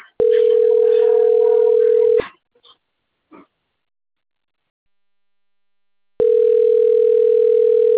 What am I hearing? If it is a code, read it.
Ringback tone (North America), 440 + 480 Hz.